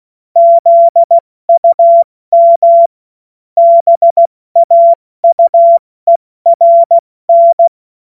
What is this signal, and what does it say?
Morse code, heard as a keyed tone, about 16 wpm, 680 Hz. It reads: ZUM BAUERN
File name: Morsekurs / Witze / Witz 07. März 2023 70 F80